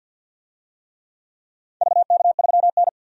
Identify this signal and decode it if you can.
Morse code, heard as a keyed tone, about 51 wpm, 700 Hz. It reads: VK3N
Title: DXCC-100 Call Signs - 4 Characters - 50wpm